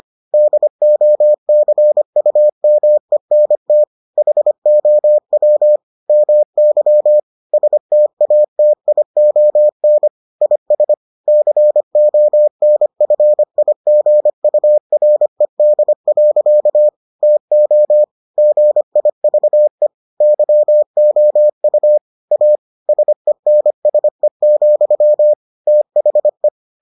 Morse code, heard as a keyed tone, about 25 wpm, 600 Hz. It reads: DOCUMENT HOW MY STATION IS CONFIGURED. TO GIVE YOU A SENSE, THE